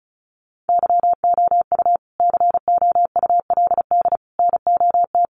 Morse code, heard as a keyed tone, about 34 wpm, 700 Hz. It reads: YOU COULD NOT
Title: Sentences from Top 1400 Words 35wpm